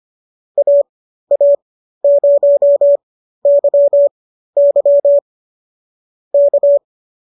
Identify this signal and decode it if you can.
Morse code, heard as a keyed tone, about 25 wpm, spaced as for 13 wpm, 590 Hz. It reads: AA0YY K
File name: qso213 13